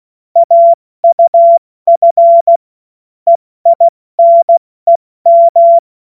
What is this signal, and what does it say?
Morse code, heard as a keyed tone, about 16 wpm, spaced as for 14 wpm, 680 Hz. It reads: AUF EINEM